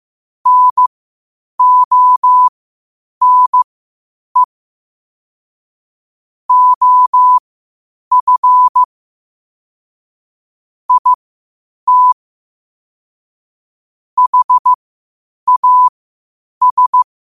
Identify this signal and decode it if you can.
Morse code, heard as a keyed tone, about 15 wpm, spaced as for 8 wpm, 1000 Hz. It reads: NONE OF IT HAS
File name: qotdfiles / 8wpm